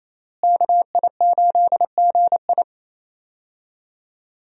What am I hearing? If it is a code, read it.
Morse code, heard as a keyed tone, about 28 wpm, 700 Hz. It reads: KI8GI
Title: USA Call Signs - 5 Characters - Repeated 2x - 28wpm